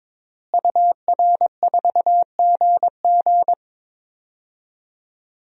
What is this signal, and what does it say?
Morse code, heard as a keyed tone, about 22 wpm, 700 Hz. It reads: UR4GG
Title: DXCC-100 Call Signs - 5 Characters - Repeated 3x - 22wpm